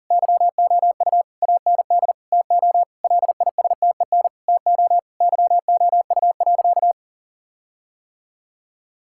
Morse code, heard as a keyed tone, about 40 wpm, 700 Hz. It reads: YOU AND TO LISTEN TO YOU.
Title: Sentences from Top 1600 Words 40wpm